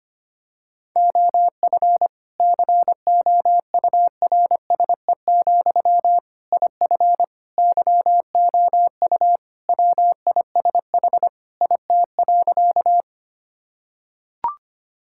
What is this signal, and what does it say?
Morse code, heard as a keyed tone, about 25 wpm, 700 Hz. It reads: OF COURSE, IF YOU WISH IT.
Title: Sentences from Top 1600 Words 25wpm